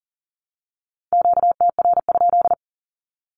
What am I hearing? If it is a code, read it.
Morse code, heard as a keyed tone, about 40 wpm, 700 Hz. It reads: QTR?
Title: Q-Codes Repeated 3x 40wpm